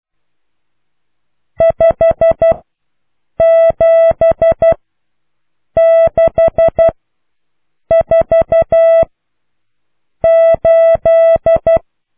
Morse code, heard as a keyed tone, about 12 wpm, spaced as for 6 wpm, 660 Hz. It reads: 57648